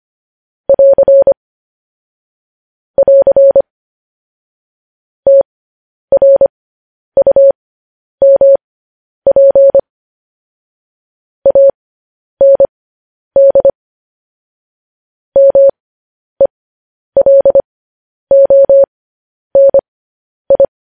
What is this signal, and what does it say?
Morse code, heard as a keyed tone, about 25 wpm, spaced as for 10 wpm, 550 Hz. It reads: <AR> <AR> TRUMP AND MELONI
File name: podcast / News headlines-1x-25x10